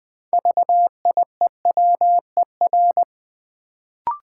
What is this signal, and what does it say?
Morse code, heard as a keyed tone, about 20 wpm, 700 Hz. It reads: VIEWER
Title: Build A Word - 2 Letters At A Time - 6-Letter Words - 20wpm